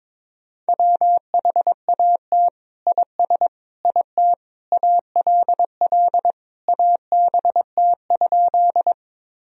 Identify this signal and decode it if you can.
Morse code, heard as a keyed tone, about 21 wpm, 700 Hz. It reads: WHAT IS IT ALL ABT?
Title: Sentences from Top 100 Words and Abbreviations 22wpm